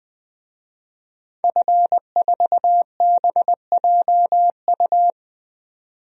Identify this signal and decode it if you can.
Morse code, heard as a keyed tone, about 20 wpm, 700 Hz. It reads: F4BJU
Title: DXCC-100 Call Signs - 5 Characters - 20wpm